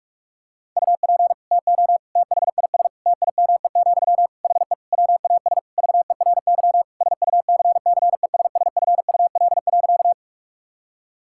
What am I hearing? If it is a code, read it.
Morse code, heard as a keyed tone, about 41 wpm, 700 Hz. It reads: UP TO THIS TIME, HE WAS VERY SUCCESSFUL.
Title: Sentences from Top 1000 Words - Review Part 6 - 45wpm